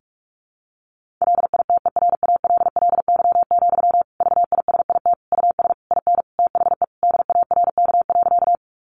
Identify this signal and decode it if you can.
Morse code, heard as a keyed tone, about 44 wpm, 700 Hz. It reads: LITERALLY, VISIT US IN THE DARK.
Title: Sentences from Top 1400 Words 45wpm